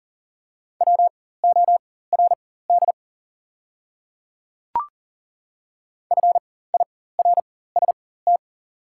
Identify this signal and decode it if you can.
Morse code, heard as a keyed tone, about 40 wpm, spaced as for 19 wpm, 700 Hz. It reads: WORD FIRST
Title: 4x Character Spacing - Top 500 Words - 40wpm